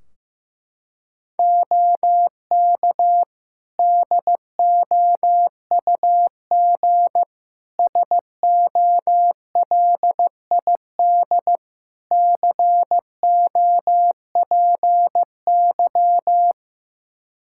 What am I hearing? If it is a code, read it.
Morse code, heard as a keyed tone, about 15 wpm, 700 Hz. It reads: OK DOUG SOLID COPY
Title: Practice QSO - Exchange 3 - More Info - 15wpm